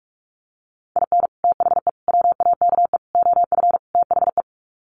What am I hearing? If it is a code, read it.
Morse code, heard as a keyed tone, about 44 wpm, 700 Hz. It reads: IN THE WAKE OF THE
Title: Sets of 5 Words 45wpm